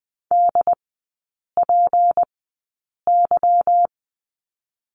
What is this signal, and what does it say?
Morse code, heard as a keyed tone, about 20 wpm, spaced as for 8 wpm, 700 Hz. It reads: DPY
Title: 152 Lesson 25e - Letter p - Three Characters 20wpm